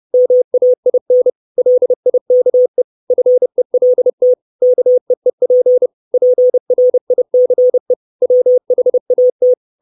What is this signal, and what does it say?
Morse code, heard as a keyed tone, about 30 wpm, 500 Hz. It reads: MAIN LIKE FELT KEEP PRICE WHAT
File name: 632 Words 30WPM